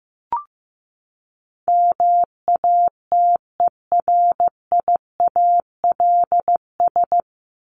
Morse code, heard as a keyed tone, about 15 wpm, 700 Hz. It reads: MATERIALS
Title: ES:S Suffix - Top 100 Words with Base Words 15wpm